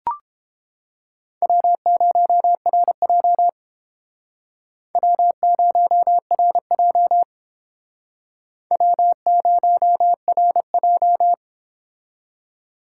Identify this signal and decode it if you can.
Morse code, heard as a keyed tone, about 30 wpm, 700 Hz. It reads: W0RJ W0RJ W0RJ